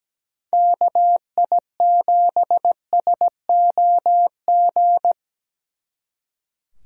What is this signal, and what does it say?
Morse code, heard as a keyed tone, about 20 wpm, 700 Hz. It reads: KI7SOG
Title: Speed Racing - USA 2x3 Call Signs 17wpm